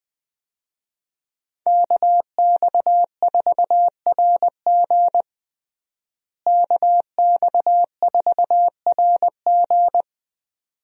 Morse code, heard as a keyed tone, about 21 wpm, 700 Hz. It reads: KX4RG KX4RG
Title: USA Call Signs - 5 Characters - Repeated 2x - 20wpm